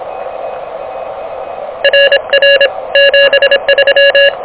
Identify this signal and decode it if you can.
Morse code, heard as a keyed tone, about 26 wpm, 1700 Hz, with noise. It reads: RR 73
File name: OK QRP 2023 / OK QRP 2023